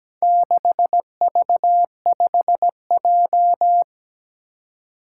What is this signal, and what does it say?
Morse code, heard as a keyed tone, about 17 wpm, 700 Hz. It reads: 6V5J
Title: Four Challenging Letter-Number Combinations 17wpm